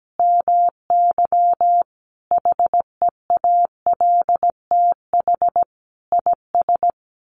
Morse code, heard as a keyed tone, about 17 wpm, 700 Hz. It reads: MY HEALTH IS
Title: Sentences from Top 1200 Words 17wpm